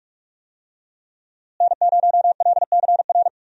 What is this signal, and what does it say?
Morse code, heard as a keyed tone, about 43 wpm, 700 Hz. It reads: N0LCR
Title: USA Call Signs - 5 Characters - Repeated 2x - 45wpm